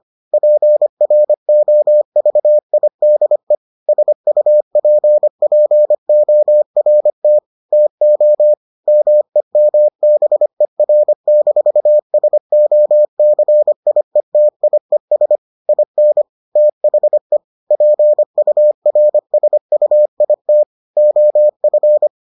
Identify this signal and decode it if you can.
Morse code, heard as a keyed tone, about 25 wpm, 600 Hz. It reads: PROVIDE SUPPORT TO MEMBER-SOCIETIES IN THE PURSUIT OF